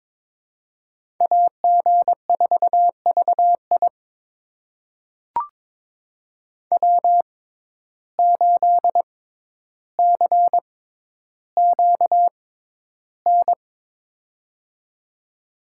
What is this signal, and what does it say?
Morse code, heard as a keyed tone, about 22 wpm, spaced as for 11 wpm, 700 Hz. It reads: AG4VI W8CQN
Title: USA Call Signs - 5 Characters - 6x Character Spacing - 22wpm